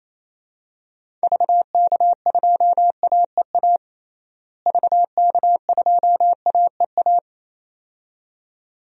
Morse code, heard as a keyed tone, about 28 wpm, 700 Hz. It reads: VK2AEA VK2AEA
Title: DXCC-100 Call Signs - 6 Characters - Repeated 2x - 28wpm